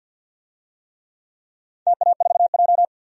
Morse code, heard as a keyed tone, about 49 wpm, 700 Hz. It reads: TA4J
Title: DXCC-100 Call Signs - 4 Characters - 50wpm